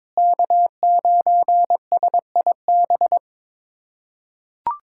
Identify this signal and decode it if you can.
Morse code, heard as a keyed tone, about 22 wpm, 700 Hz. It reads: K9SIB
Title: USA Callsign DE Callsign - 5 Characters - 22wpm